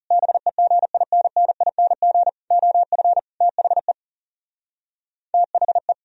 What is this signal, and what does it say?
Morse code, heard as a keyed tone, about 36 wpm, 700 Hz. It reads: BEGINNING OF THE THE
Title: Speed-Racing - Sets of 4 Words 30wpm